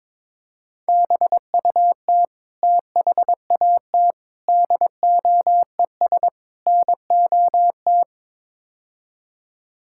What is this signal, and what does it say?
Morse code, heard as a keyed tone, about 22 wpm, 700 Hz. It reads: BUT THAT DOES NOT
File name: practice / 22wpm / Build A Phrase - 5-Word Phrases - 22wpm